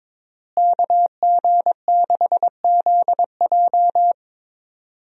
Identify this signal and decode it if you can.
Morse code, heard as a keyed tone, about 22 wpm, 700 Hz. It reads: KG6ZJ
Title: USA Call Signs - 5 Characters - Repeated 3x - 22wpm